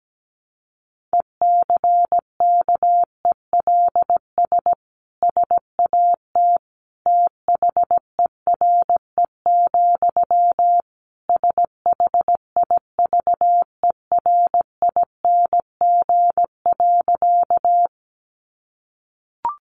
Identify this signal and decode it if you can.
Morse code, heard as a keyed tone, about 17 wpm, 700 Hz. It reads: ECKELS SAT THERE, SHIVERING.
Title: A Sound Of Thunder 17wpm